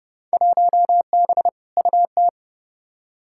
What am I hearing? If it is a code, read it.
Morse code, heard as a keyed tone, about 30 wpm, 700 Hz. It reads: 1B UT
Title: Field Day Exchanges - 30wpm